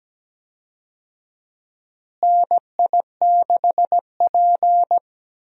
Morse code, heard as a keyed tone, about 17 wpm, 700 Hz. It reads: NI6P
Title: USA Call Signs - 4 Characters - Repeated 2x - 17wpm